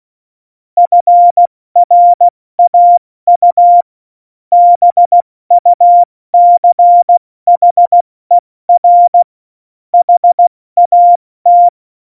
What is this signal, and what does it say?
Morse code, heard as a keyed tone, about 16 wpm, 700 Hz. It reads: FRAU BUCHER HAT
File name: Morsekurs / Witze / Witz 15. August 17, Tempo 70 F80